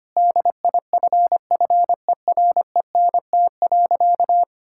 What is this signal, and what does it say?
Morse code, heard as a keyed tone, about 25 wpm, 700 Hz. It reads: DIFFERENT.
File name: practice / 25wpm / Sentences with 4-Word Phrases - 25wpm